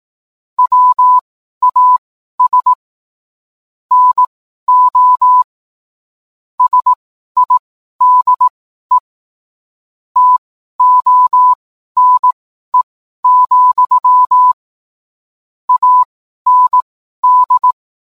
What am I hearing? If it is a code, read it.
Morse code, heard as a keyed tone, about 18 wpm, spaced as for 12 wpm, 1000 Hz. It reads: WAS NO SIDE TONE, AND